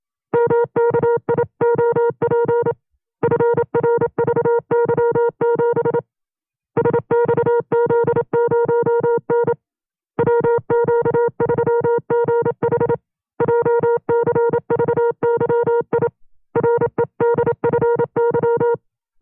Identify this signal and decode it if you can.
Morse code, heard as a keyed tone, about 27 wpm, 470 Hz. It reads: MKIOP FRVY7 SXZ0N WQ3GH JCVYI REDFY
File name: FreeFiles / CW-practice / 25WPM-side1